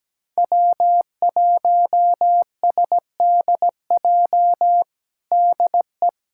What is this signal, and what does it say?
Morse code, heard as a keyed tone, about 17 wpm, 700 Hz. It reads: W1SDJ DE
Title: USA Callsign DE Callsign - 5 Characters - 17wpm